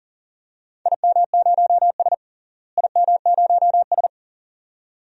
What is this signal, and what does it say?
Morse code, heard as a keyed tone, about 40 wpm, 700 Hz. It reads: IM0S IM0S
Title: DXCC-100 Call Signs - 4 Characters - Repeated 2x - 40wpm